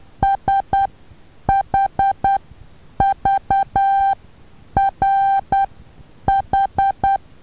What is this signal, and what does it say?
Morse code, heard as a keyed tone, about 10 wpm, spaced as for 8 wpm, 790 Hz. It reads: SHVRH
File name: Taviro HA6VA B